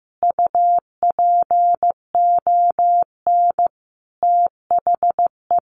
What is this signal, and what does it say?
Morse code, heard as a keyed tone, about 15 wpm, 700 Hz. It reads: UPON THE
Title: A Sound Of Thunder 15wpm